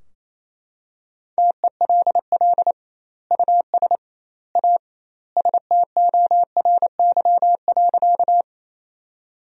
Morse code, heard as a keyed tone, about 28 wpm, 700 Hz. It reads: TELL US A STORY.
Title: 2x Word-Spacing - Sentences from Top 300 Words - 28wpm